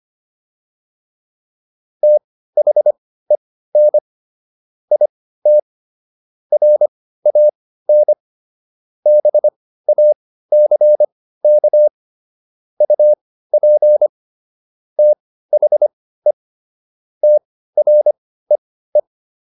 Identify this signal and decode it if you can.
Morse code, heard as a keyed tone, about 25 wpm, spaced as for 15 wpm, 600 Hz. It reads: THEN IT RAN BACK UP THE TREE